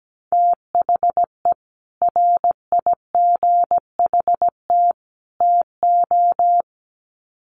Morse code, heard as a keyed tone, about 17 wpm, 700 Hz. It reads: THE RIGHT TO
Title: Sets of 3 Words 17wpm